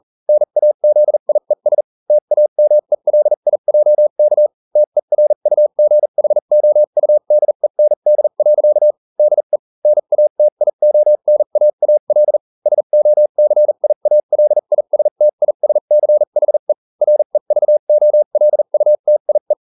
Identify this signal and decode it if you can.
Morse code, heard as a keyed tone, about 40 wpm, spaced as for 35 wpm, 600 Hz. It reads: NAZIES TAMELIJK TERUGHOUDEND. DE NATIONAAL SOCIALISTISCHE REVOLUTIE